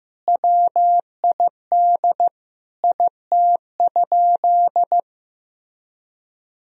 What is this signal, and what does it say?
Morse code, heard as a keyed tone, about 15 wpm, 700 Hz. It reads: WID IT?
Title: Sentences from Top 100 Words and Abbreviations 15wpm